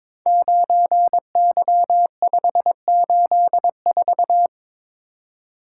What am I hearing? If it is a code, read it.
Morse code, heard as a keyed tone, about 22 wpm, 700 Hz. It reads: 9Y584
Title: Code Groups - 5 Letter:Numbers - 22wpm